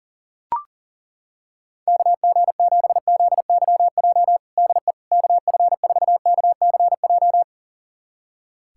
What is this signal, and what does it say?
Morse code, heard as a keyed tone, about 40 wpm, 700 Hz. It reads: KG7ZYJ DE KF4KCJ
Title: USA Callsign DE Callsign - 6 Characters - 40wpm